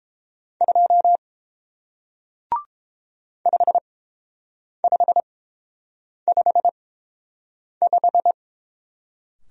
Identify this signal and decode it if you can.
Morse code, heard as a keyed tone, about 31 wpm, spaced as for 7 wpm, 700 Hz. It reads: 2 5555